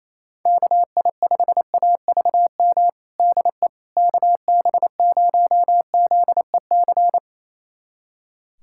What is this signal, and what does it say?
Morse code, heard as a keyed tone, about 28 wpm, 700 Hz. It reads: KI5AVM DE KB0ZEC
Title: USA Callsign DE Callsign - 6 Characters - 28wpm